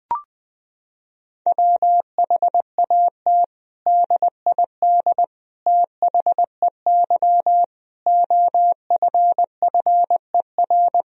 Morse code, heard as a keyed tone, about 20 wpm, 700 Hz. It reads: WHAT DID THEY OFFER
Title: Sentences from Top 1000 Words - Review Part 4 - 20wpm